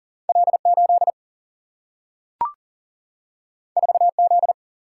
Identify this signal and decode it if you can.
Morse code, heard as a keyed tone, about 40 wpm, 700 Hz. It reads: L 8 4 Z
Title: Single-Letter Single-Number 40wpm